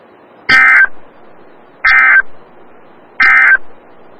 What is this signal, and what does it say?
Emergency Alert System (SAME) tones, heard as harsh buzzing bursts. It is heard from 0.4 to 3.7 s.
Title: Des Moines, IA NOAA Weather Radio - WXL57